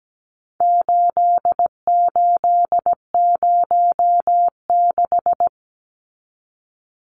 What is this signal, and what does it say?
Morse code, heard as a keyed tone, about 17 wpm, 700 Hz. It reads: 8806
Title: Numbers - 4-Digit 17wpm